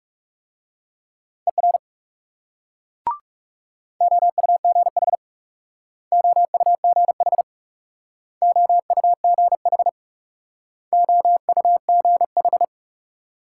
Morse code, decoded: ER OUGH OUGH OUGH OUGH

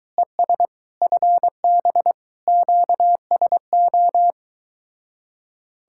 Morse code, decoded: ES FB QSO

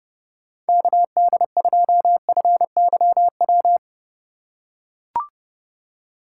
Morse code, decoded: KD2FYW

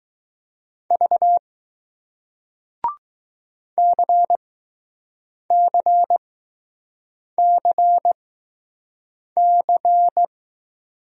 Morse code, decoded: V CCCC

